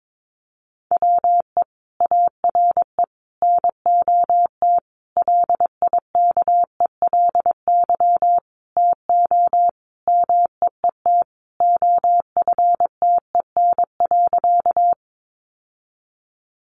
Morse code, heard as a keyed tone, about 22 wpm, 700 Hz. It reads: WE ARE NOT LIKELY TO MEET OFTEN.
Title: Sentences from Top 1100 Words 22wpm